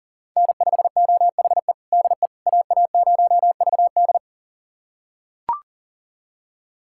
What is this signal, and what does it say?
Morse code, heard as a keyed tone, about 40 wpm, 700 Hz. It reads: N5OHE DE AA0VD
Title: USA Callsign DE Callsign - 5 Characters - 40wpm